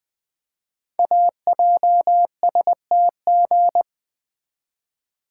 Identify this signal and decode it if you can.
Morse code, heard as a keyed tone, about 20 wpm, 700 Hz. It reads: AJSTG
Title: Code Groups - 5 Letters - 20wpm